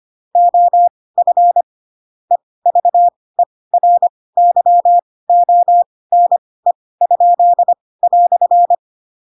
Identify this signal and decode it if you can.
Morse code, heard as a keyed tone, about 25 wpm, spaced as for 18 wpm, 700 Hz. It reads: OF EVERYONE?"
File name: Pinocchio / Pinocchio ch4 12 25 ik1hge